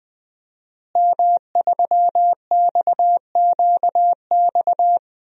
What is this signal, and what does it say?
Morse code, heard as a keyed tone, about 21 wpm, 700 Hz. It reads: M3XQX
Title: DXCC-100 Call Signs - 5 Characters - Repeated 3x - 20wpm